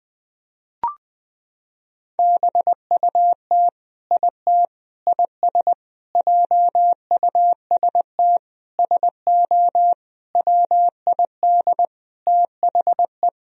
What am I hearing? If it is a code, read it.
Morse code, heard as a keyed tone, about 20 wpm, 700 Hz. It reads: BUT IT IS JUST SO WID THE
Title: Sentences from Top 200 Words and Abbreviations 20wpm